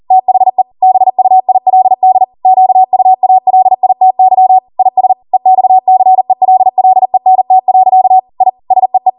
Morse code, heard as a keyed tone, about 40 wpm, 750 Hz. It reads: THE BUILD QUALITY IS EXCELLENT. I SEE
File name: Morse / 40WPM